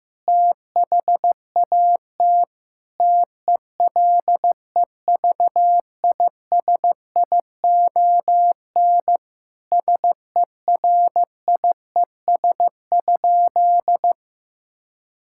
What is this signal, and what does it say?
Morse code, heard as a keyed tone, about 15 wpm, 700 Hz. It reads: THAT TELEVISION SERIES?
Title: Sentences from Top 1000 Words - Review Part 5 - 15wpm